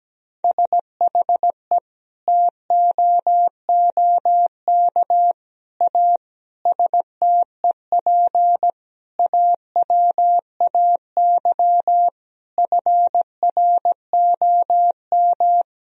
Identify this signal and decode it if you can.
Morse code, heard as a keyed tone, about 17 wpm, 700 Hz. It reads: SHE TOOK A STEP AWAY FROM
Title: Sentences from Top 1000 Words - Review Part 2 - 17wpm